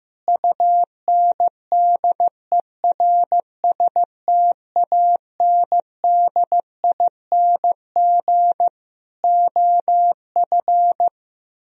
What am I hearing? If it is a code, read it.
Morse code, heard as a keyed tone, about 15 wpm, 700 Hz. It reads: UNDERSTANDING OF
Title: Sentences from Top 1000 Words - Review Part 3 - 15wpm